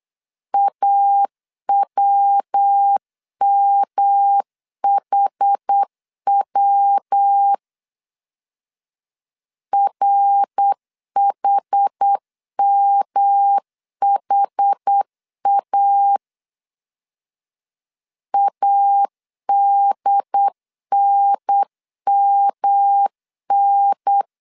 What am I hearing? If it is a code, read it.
Morse code, heard as a keyed tone, about 8 wpm, 800 Hz. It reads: AWMHW RHMHA ADNMN
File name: CW / DET / 1 3 1500C S B 8wpm 4141